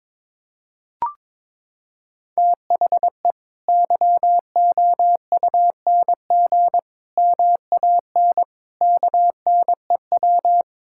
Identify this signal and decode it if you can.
Morse code, heard as a keyed tone, about 22 wpm, 700 Hz. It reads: THE YOUNG MAN KNEW
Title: Sentences from Top 1000 Words - Review Part 6 - 22wpm